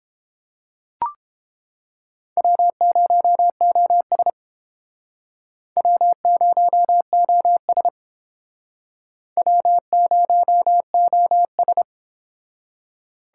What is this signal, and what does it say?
Morse code, heard as a keyed tone, about 30 wpm, 700 Hz. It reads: W0OS W0OS W0OS